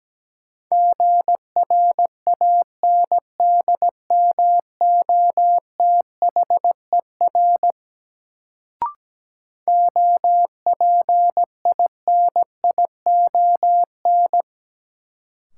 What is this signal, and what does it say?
Morse code, heard as a keyed tone, about 17 wpm, 700 Hz. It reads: GRANDMOTHER OPINION
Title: Sentences from Top 1200 Words - New Word Drill - 17wpm